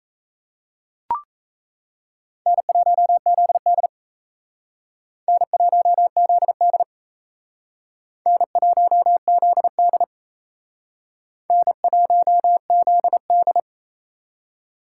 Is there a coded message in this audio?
Yes: Morse code from 2.5 to 13.6 s, heard as a keyed tone, about 35 wpm, 700 Hz. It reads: N1ZD N1ZD N1ZD N1ZD